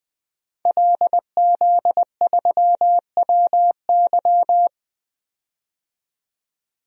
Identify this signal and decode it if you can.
Morse code, heard as a keyed tone, about 20 wpm, 700 Hz. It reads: LZ3WY
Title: DXCC-100 Call Signs - 5 Characters - 20wpm